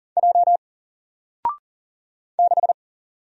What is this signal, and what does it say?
Morse code, heard as a keyed tone, about 40 wpm, 700 Hz. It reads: J 6
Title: Single Letter-Number Alt 40wpm